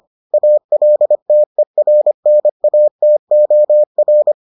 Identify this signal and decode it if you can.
Morse code, heard as a keyed tone, about 25 wpm, 600 Hz. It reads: ALTERNATOR